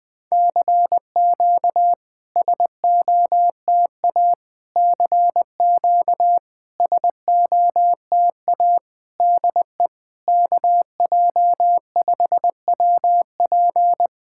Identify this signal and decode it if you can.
Morse code, heard as a keyed tone, about 21 wpm, 700 Hz. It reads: CQ SOTA CQ SOTA DE KJ5WP
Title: SOTA QSO Practice 20wpm